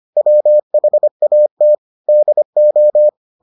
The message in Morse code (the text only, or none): WHAT DO